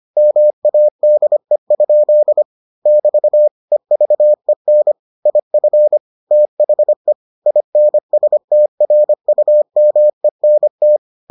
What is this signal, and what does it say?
Morse code, heard as a keyed tone, about 25 wpm, 600 Hz. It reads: MADE? <BT> EVEN IF THE INSTRUMENT